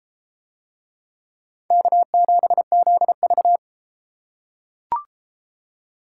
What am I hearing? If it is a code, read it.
Morse code, heard as a keyed tone, about 33 wpm, 700 Hz. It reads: K7ZV